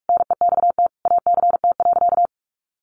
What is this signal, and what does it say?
Morse code, heard as a keyed tone, about 44 wpm, 700 Hz. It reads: NEXT ACT.
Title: Sentences from Top 900 Words 45wpm